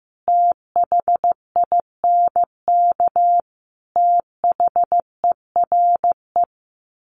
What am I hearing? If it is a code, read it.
Morse code, heard as a keyed tone, about 15 wpm, 700 Hz. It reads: THINK THERE
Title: Sets of 5 Words 15wpm